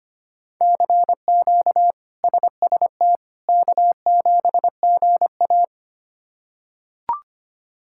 Morse code, heard as a keyed tone, about 25 wpm, 700 Hz. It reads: CQ SST K7GA
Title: K1USN SST QSO Practice - 25wpm